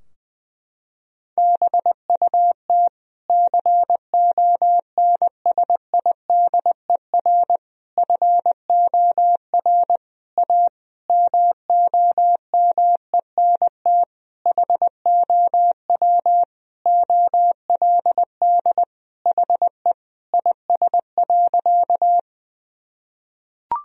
Morse code, heard as a keyed tone, about 20 wpm, 700 Hz. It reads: BUT CONSIDER FOR A MOMENT HOW OLD HE IS.
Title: Sentences from Top 1000 Words - Review Part 4 - 20wpm